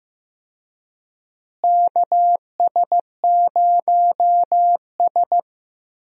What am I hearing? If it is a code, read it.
Morse code, heard as a keyed tone, about 15 wpm, 700 Hz. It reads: KS0S
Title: USA Call Signs - 4 Characters - Repeated 2x - 15wpm